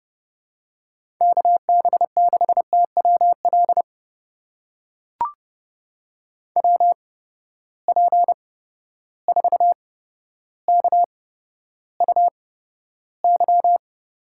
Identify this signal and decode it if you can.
Morse code, heard as a keyed tone, about 30 wpm, spaced as for 13 wpm, 700 Hz. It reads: KB6TWL WP4KUY
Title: USA Call Signs - 6 Characters - 8x Character Spacing - 30wpm